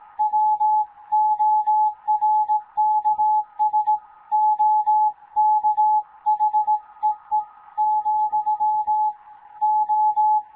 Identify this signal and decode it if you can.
Morse code, heard as a keyed tone, about 18 wpm, 860 Hz. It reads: WORKS OK HEE , O
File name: Morze-HA2MN / hangp / 2 11 GYAKORLAT